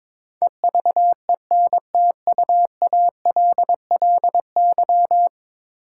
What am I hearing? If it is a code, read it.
Morse code, heard as a keyed tone, about 22 wpm, 700 Hz. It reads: EVENTUALLY
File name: practice / 22wpm / 10 Letter Words - First 3 Letters Given - 22wpm